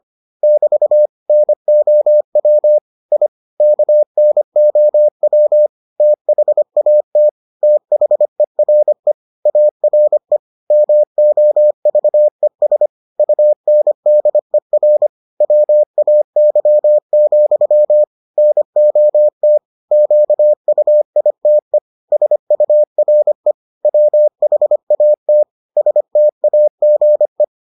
Morse code, heard as a keyed tone, about 25 wpm, 600 Hz. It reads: <BT> NOW I KNOW THAT THERE ARE MOVES UNDER WAY, NOT QUITE SURE WHAT STAGE